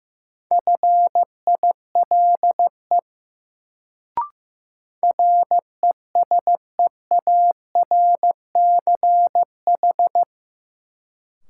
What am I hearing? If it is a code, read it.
Morse code, heard as a keyed tone, about 15 wpm, 700 Hz. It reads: FILE RESEARCH
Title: Sentences from Top 1000 Words - Review Part 4 - New Word Drill - 15wpm